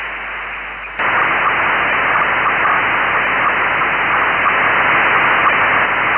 SSTV (slow-scan television), PD 240.